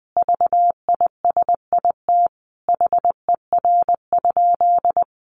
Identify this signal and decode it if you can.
Morse code, heard as a keyed tone, about 20 wpm, 700 Hz. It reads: VISIT HER?
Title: Sentences from Top 900 Words 20wpm